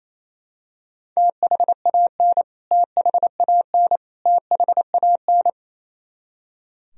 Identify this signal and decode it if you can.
Morse code, heard as a keyed tone, about 28 wpm, 700 Hz. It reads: THAN THAN THAN
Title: Top 100 Words Repeated 28wpm